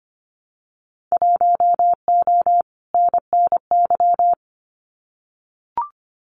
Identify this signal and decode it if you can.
Morse code, heard as a keyed tone, about 25 wpm, 700 Hz. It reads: 1O NNY
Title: Winter Field Day Exchanges 25wpm